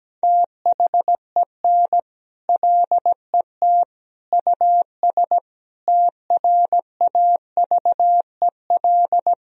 Morse code, heard as a keyed tone, about 17 wpm, 700 Hz. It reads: THEN LET US TRAVEL
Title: Sentences from Top 1000 Words - Review Part 2 - 17wpm